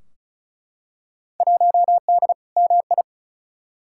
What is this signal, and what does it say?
Morse code, heard as a keyed tone, about 36 wpm, 700 Hz. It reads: 1D MI
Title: Field Day Exchanges - 35wpm